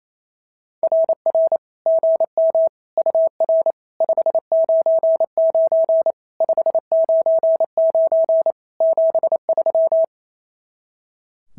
Morse code, heard as a keyed tone, about 27 wpm, 650 Hz. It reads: RR GM UR 599 599 73
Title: SOTA QSO Practice 28wpm